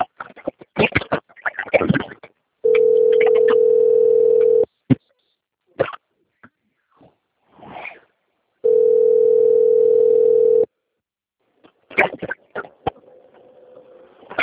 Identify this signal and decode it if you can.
Ringback tone (North America), 440 + 480 Hz.